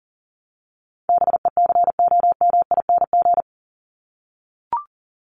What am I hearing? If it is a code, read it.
Morse code, heard as a keyed tone, about 40 wpm, 700 Hz. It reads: BECOMING
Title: ING Suffix - Top 100 Words with Base Words 40wpm